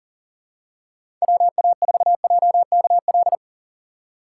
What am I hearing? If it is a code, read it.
Morse code, heard as a keyed tone, about 40 wpm, 700 Hz. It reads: WA4JKL
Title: K1USN SST QSO Practice - 40wpm